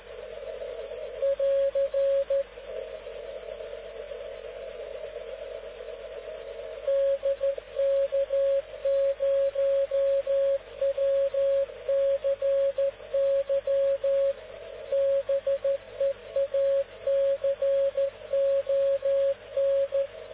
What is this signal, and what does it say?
Morse code, heard as a keyed tone, about 14 wpm, 550 Hz. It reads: <AR> DK0WCY BEACON